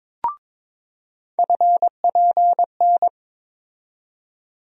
Morse code, heard as a keyed tone, about 23 wpm, 700 Hz. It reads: FPN
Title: USA Call Sign Postfixes - 3 Letters 22wpm